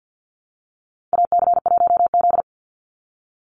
Morse code, heard as a keyed tone, about 50 wpm, 700 Hz. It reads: AC1Z
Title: NA SOTA Activator Callsigns 50wpm